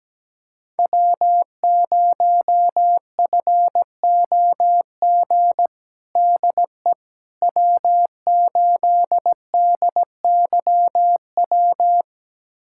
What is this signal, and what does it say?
Morse code, heard as a keyed tone, about 17 wpm, 700 Hz. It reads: W0FOG DE W8DYW
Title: USA Callsign DE Callsign - 5 Characters - 17wpm